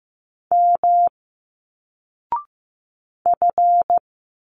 Morse code, heard as a keyed tone, about 15 wpm, 700 Hz. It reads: M F
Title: Single Letters 15wpm